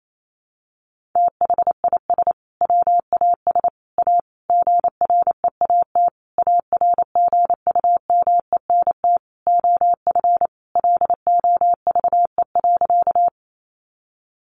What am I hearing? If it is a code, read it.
Morse code, heard as a keyed tone, about 28 wpm, 700 Hz. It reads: THIS WAS A GREAT ARGUMENT OF LOVE.
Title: Sentences from Top 1400 Words 28wpm